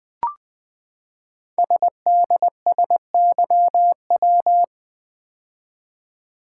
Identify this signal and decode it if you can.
Morse code, heard as a keyed tone, about 20 wpm, 700 Hz. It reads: SDSYW
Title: Code Groups - 5 Letter:Numbers - 20wpm